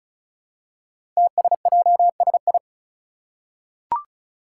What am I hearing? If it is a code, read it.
Morse code, heard as a keyed tone, about 36 wpm, 700 Hz. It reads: TSJSI
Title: Five Challenging Letter-Number Combinations 35wpm